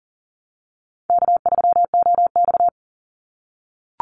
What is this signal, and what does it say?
Morse code, heard as a keyed tone, about 38 wpm, 700 Hz. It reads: K3OX
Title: Easy Call Signs - 40wpm